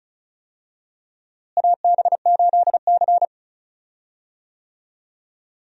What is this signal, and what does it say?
Morse code, heard as a keyed tone, about 36 wpm, 700 Hz. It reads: AB8C
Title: USA Call Signs - 4 Characters - Repeated 2x - 35wpm